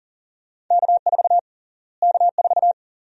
Morse code, heard as a keyed tone, about 40 wpm, 700 Hz. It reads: K4 K4
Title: USA Call Sign Prefixes - Repeated 2x - 40wpm